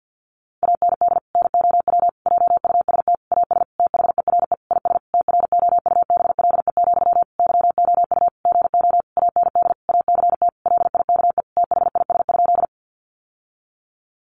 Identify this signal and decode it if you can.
Morse code, heard as a keyed tone, about 48 wpm, 700 Hz. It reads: AND NOW JUST AS THERE IS TROUBLE, YOU GO AND ACT LIKE THIS?
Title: Sentences from Top 1200 Words 50wpm